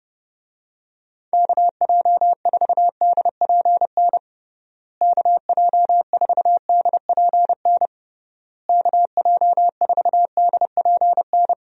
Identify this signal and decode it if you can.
Morse code, heard as a keyed tone, about 30 wpm, 700 Hz. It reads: KJ4DPN KJ4DPN KJ4DPN
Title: USA Call Signs - 6 Characters - Repeated 3x - 30wpm